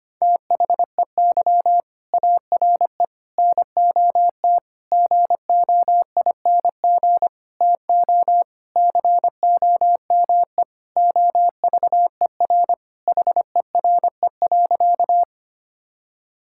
Morse code, heard as a keyed tone, about 25 wpm, 700 Hz. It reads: THEY ARE NOT GOING TO COME OVER HERE.
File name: practice / 25wpm / Sentences with 4-Word Phrases - 25wpm